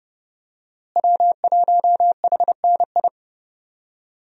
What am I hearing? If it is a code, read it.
Morse code, heard as a keyed tone, about 30 wpm, 700 Hz. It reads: W1HNI
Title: USA Call Signs - 5 Characters - Repeated 2x - 30wpm